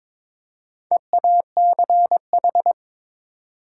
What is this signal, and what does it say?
Morse code, heard as a keyed tone, about 21 wpm, 700 Hz. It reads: EACH
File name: Build A Phrase - 2-Word Phrases - 22wpm